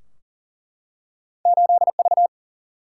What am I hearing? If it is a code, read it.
Morse code, heard as a keyed tone, about 41 wpm, 700 Hz. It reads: 8V
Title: Single-Letter Single-Number 40wpm